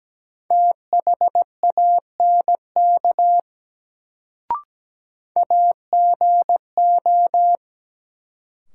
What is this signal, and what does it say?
Morse code, heard as a keyed tone, about 17 wpm, 700 Hz. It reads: THANK AGO
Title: Sets of 2 Words - New Word Drill - 17wpm